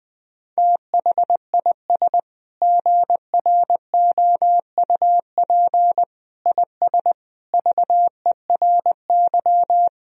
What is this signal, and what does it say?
Morse code, heard as a keyed tone, about 21 wpm, 700 Hz. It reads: THIS GROUP IS VERY
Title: Sentences from Top 1000 Words - Review Part 1 - 20wpm